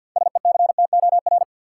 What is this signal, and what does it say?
Morse code, heard as a keyed tone, about 53 wpm, 700 Hz. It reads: SECTOR
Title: Six Letter Words - 50wpm